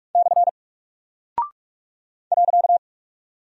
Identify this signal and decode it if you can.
Morse code, heard as a keyed tone, about 46 wpm, 700 Hz. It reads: / .
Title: Single Character (Letter:Number:Punctuation:Prosign) Alt 45wpm